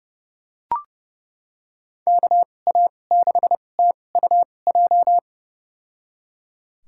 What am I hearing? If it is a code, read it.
Morse code, heard as a keyed tone, about 32 wpm, spaced as for 22 wpm, 700 Hz. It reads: KA6TUJ
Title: USA Call Signs - 6 Characters - 2x Character Spacing - 30wpm